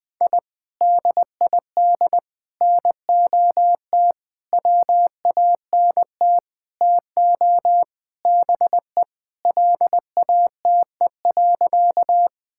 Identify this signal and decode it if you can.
Morse code, heard as a keyed tone, about 21 wpm, 700 Hz. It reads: I DID NOT WANT TO BE LATE.
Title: Sentences with 5-Word Phrases - 20wpm